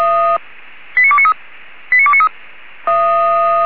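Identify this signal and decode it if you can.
Five-tone selective call (ZVEI), heard as quick runs of five tones, tones of about 70 ms. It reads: 89293 89293
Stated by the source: ZVEI five-tone call 89288 received with an SDR by fire brigade Feuerwehr Orlamünde.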